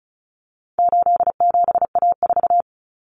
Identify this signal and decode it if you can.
Morse code, heard as a keyed tone, about 34 wpm, 700 Hz. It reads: 87A4
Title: Common Cut Numbers - 4-Digit 35wpm